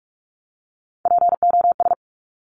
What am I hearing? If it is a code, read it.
Morse code, heard as a keyed tone, about 46 wpm, 690 Hz. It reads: POS
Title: Top Three Letter Combinations in Words 45wpm